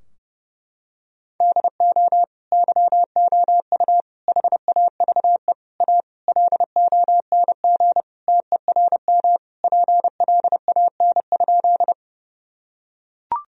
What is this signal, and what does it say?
Morse code, heard as a keyed tone, about 30 wpm, 700 Hz. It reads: DO YOU HAVE A LONG TERM PLAN?
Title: Sentences from Top 1000 Words - Review Part 4 - 30wpm